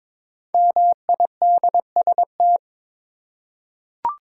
Morse code, heard as a keyed tone, about 22 wpm, 700 Hz. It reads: MIDST
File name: 5 Letter Words - First 3 Letters Given - 22wpm